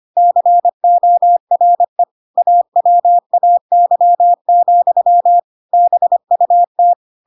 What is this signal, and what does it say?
Morse code, heard as a keyed tone, about 25 wpm, 700 Hz. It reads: CORE AWAY, BUT
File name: Pinocchio / Pinocchio ch7 25 25 ik1hge